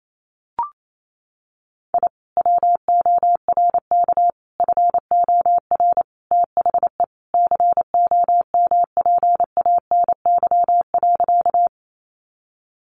Morse code, heard as a keyed tone, about 28 wpm, 700 Hz. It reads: I WORK FOR THE COMPANY.
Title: Sentences from Top 600 Words 28wpm